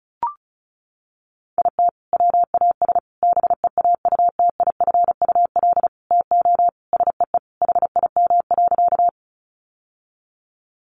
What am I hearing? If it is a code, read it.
Morse code, heard as a keyed tone, about 34 wpm, 700 Hz. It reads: IT WAS BEAUTIFUL TO SEE HIM.
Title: Sentences from Top 900 Words 35wpm